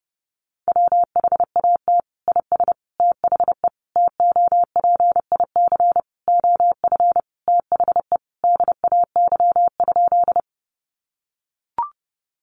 Morse code, heard as a keyed tone, about 30 wpm, 700 Hz. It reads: WHAT IS THE TOPIC OF THE DAY?
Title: Sentences from Top 1200 Words 30wpm